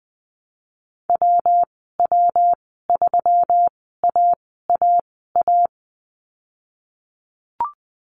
Morse code, heard as a keyed tone, about 20 wpm, spaced as for 15 wpm, 700 Hz. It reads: WW3AAA
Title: 122 Lesson 20g - Number 3 - Call Signs 20wpm